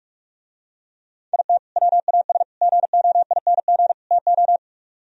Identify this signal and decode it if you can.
Morse code, heard as a keyed tone, about 44 wpm, 700 Hz. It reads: IT WAS GOING TO